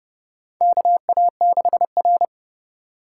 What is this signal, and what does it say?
Morse code, heard as a keyed tone, about 30 wpm, 700 Hz. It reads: KA6R
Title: USA Call Signs - 3x Character Spacing - 30wpm